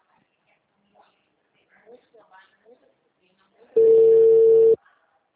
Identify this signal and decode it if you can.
Ringback tone (China, 450 Hz), with speech.